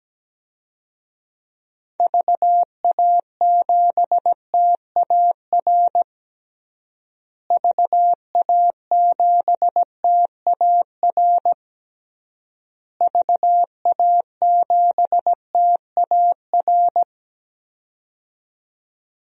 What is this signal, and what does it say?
Morse code, heard as a keyed tone, about 17 wpm, 700 Hz. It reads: VA7TAR VA7TAR VA7TAR
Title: DXCC-100 Call Signs - 6 Characters - Repeated 3x - 17wpm